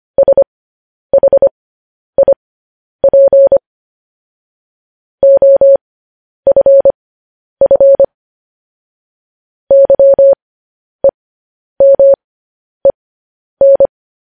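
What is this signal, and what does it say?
Morse code, heard as a keyed tone, about 25 wpm, spaced as for 10 wpm, 550 Hz. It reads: SHIP OFF YEMEN
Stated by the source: Morse code news headlines sent at 10 wpm.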